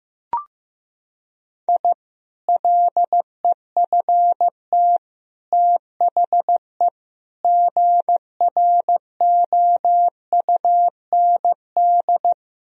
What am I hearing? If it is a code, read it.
Morse code, heard as a keyed tone, about 15 wpm, 700 Hz. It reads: I LEFT THE GROUND
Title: Sentences from Top 1000 Words - Review Part 2 - 15wpm